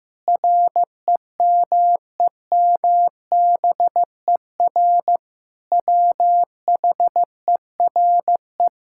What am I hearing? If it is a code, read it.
Morse code, heard as a keyed tone, about 16 wpm, 700 Hz. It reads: REMEMBER WHERE
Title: Sentences from Top 1000 Words - Review Part 6 - 15wpm